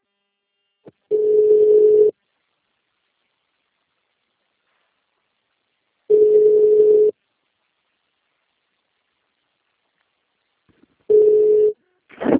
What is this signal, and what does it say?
Ringback tone (Europe, 425 Hz).